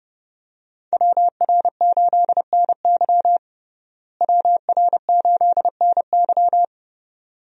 Morse code, heard as a keyed tone, about 30 wpm, 700 Hz. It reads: WR8NY WR8NY
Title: USA Call Signs - 5 Characters - Repeated 2x - 30wpm